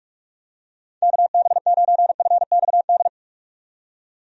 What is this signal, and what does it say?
Morse code, heard as a keyed tone, about 43 wpm, 700 Hz. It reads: KB9FXD
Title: USA Call Signs - 6 Characters - 45wpm